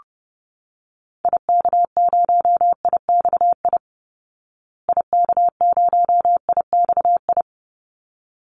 Morse code, heard as a keyed tone, about 30 wpm, 700 Hz. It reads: IK0IXI IK0IXI
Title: Call Signs Repeated 3x - 30wpm